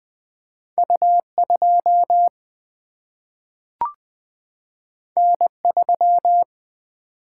Morse code, heard as a keyed tone, about 20 wpm, 700 Hz. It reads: U 2 N 3
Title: Single-Letter Single-Number 20wpm